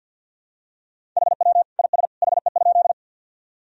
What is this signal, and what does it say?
Morse code, heard as a keyed tone, about 51 wpm, 700 Hz. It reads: HW IS HE?
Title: Sentences from Top 100 Words and Abbreviations 50wpm